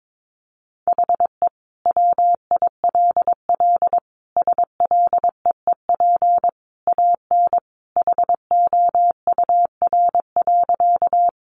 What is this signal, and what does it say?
Morse code, heard as a keyed tone, about 22 wpm, 700 Hz. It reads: HE WILL SLEEP AN HOUR.
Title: Sentences from Top 500 Words 22wpm